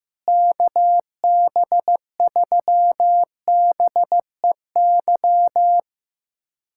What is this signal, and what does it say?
Morse code, heard as a keyed tone, about 15 wpm, 700 Hz. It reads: KB3BEY
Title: USA Callsign DE Callsign - 6 Characters - 15wpm